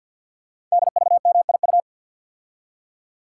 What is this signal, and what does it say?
Morse code, heard as a keyed tone, about 48 wpm, 700 Hz. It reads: DVMIU
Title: Code Groups - 5 Letters - 50wpm